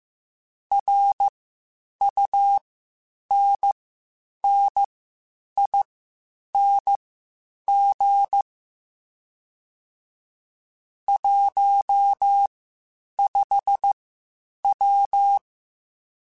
Morse code, heard as a keyed tone, about 15 wpm, spaced as for 8 wpm, 800 Hz. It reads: RUNNING 15W